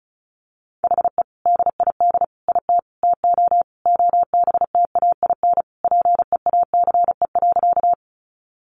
Morse code, heard as a keyed tone, about 34 wpm, 700 Hz. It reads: HE DID IT TO OBTAIN PEACE.
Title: Sentences from Top 1500 Words 35wpm